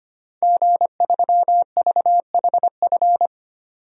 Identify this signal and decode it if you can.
Morse code, heard as a keyed tone, about 25 wpm, 700 Hz. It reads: G3VHF
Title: DXCC-100 Call Signs - 5 Characters - 25wpm